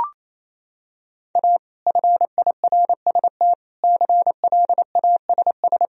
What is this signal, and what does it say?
Morse code, heard as a keyed tone, about 28 wpm, 700 Hz. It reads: A FIRST CLASS
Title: Sentences from Top 1000 Words - Review Part 2 - 28wpm